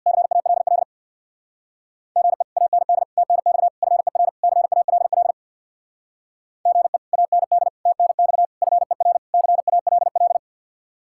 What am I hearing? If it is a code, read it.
Morse code, heard as a keyed tone, about 64 wpm, 700 Hz. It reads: CALL GE AND TNX FER CALL GE AND TNX FER CALL